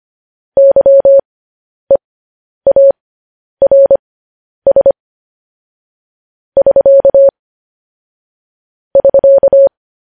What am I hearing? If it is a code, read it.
Morse code, heard as a keyed tone, about 25 wpm, spaced as for 10 wpm, 550 Hz. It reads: YEARS <SK> <SK>
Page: news headlines in morse code at 10 wpm